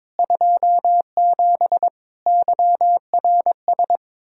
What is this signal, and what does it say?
Morse code, heard as a keyed tone, about 22 wpm, 700 Hz. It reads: 27 YRS
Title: Practice QSO - Exchange 3 - More Info - 22wpm